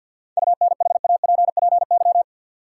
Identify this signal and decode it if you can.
Morse code, heard as a keyed tone, about 49 wpm, 700 Hz. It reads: UNHAPPY
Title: Seven Letter Words - 50wpm